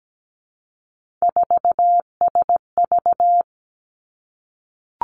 Morse code, heard as a keyed tone, about 17 wpm, 700 Hz. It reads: 4SV
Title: Three Challenging Letter-Number Combinations 17wpm